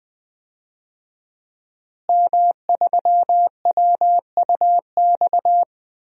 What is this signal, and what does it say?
Morse code, heard as a keyed tone, about 20 wpm, 700 Hz. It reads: M3WUX
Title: DXCC-100 Call Signs - 5 Characters - Repeated 3x - 20wpm